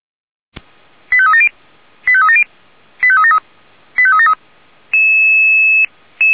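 Five-tone selective call (ZVEI), heard as quick runs of five tones, tones of about 70 ms. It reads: 85370 85370 85373 85373
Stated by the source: ZVEI five-tone selective call received on SDR.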